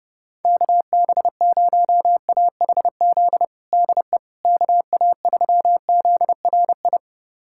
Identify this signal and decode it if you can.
Morse code, heard as a keyed tone, about 30 wpm, 700 Hz. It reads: KB0AHZ DE KA3ZRI
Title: USA Callsign DE Callsign - 6 Characters - 30wpm